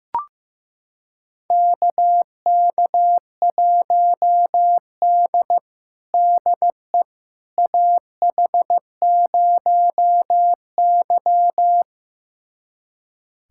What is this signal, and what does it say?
Morse code, heard as a keyed tone, about 15 wpm, 700 Hz. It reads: KK1D DE AH0Y
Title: USA Callsign DE Callsign - 4 Characters - 15wpm